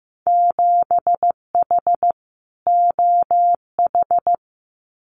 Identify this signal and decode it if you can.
Morse code, heard as a keyed tone, about 15 wpm, 700 Hz. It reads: 7H OH
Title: Winter Field Day Exchanges Encore 15wpm